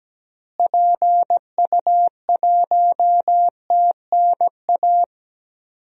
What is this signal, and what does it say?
Morse code, heard as a keyed tone, about 17 wpm, 700 Hz. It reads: PU1TNA
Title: DXCC-100 Call Signs - 6 Characters - Repeated 2x - 17wpm